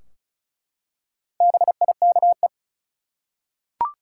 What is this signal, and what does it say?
Morse code, heard as a keyed tone, about 36 wpm, 700 Hz. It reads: BIKE
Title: Four Letter Words - 35wpm